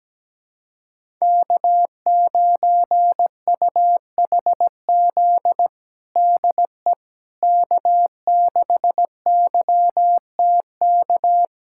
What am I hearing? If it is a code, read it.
Morse code, heard as a keyed tone, about 17 wpm, 700 Hz. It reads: K9UHZ DE K6YTK
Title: USA Callsign DE Callsign - 5 Characters - 17wpm